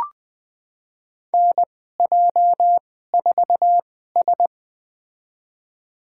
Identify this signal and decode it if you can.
Morse code, heard as a keyed tone, about 20 wpm, spaced as for 14 wpm, 700 Hz. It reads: NJ4S
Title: USA Call Signs - 4 Characters - 2x Character Spacing - 20wpm